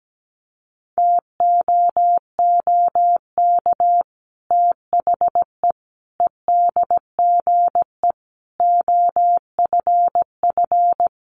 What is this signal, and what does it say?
Morse code, heard as a keyed tone, about 17 wpm, 700 Hz. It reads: TOOK THE EDGE OFF
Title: Sentences from Top 1300 Words 17wpm